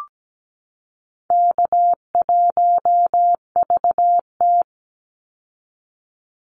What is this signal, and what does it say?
Morse code, heard as a keyed tone, about 17 wpm, 700 Hz. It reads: K1VT
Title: Easy Call Signs Repeated 2x - 17wpm